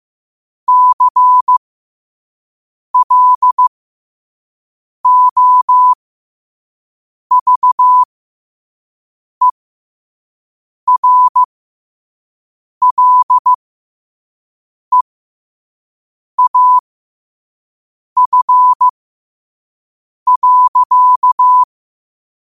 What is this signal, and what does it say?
Morse code, heard as a keyed tone, about 15 wpm, spaced as for 5 wpm, 1000 Hz. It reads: CLOVERLEAF.
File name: qotdfiles / 5wpm